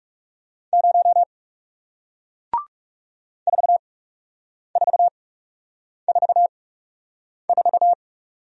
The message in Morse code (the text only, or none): 0 4444